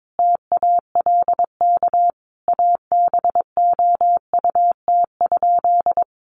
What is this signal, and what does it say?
Morse code, heard as a keyed tone, about 22 wpm, 700 Hz. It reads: TALK ABOUT?
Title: Sentences from Top 400 Words 22wpm